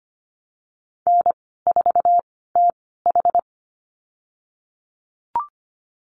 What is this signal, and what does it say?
Morse code, heard as a keyed tone, about 25 wpm, spaced as for 16 wpm, 700 Hz. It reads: N4TH
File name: Lesson 15g - Number 5 - Call Signs 25wpm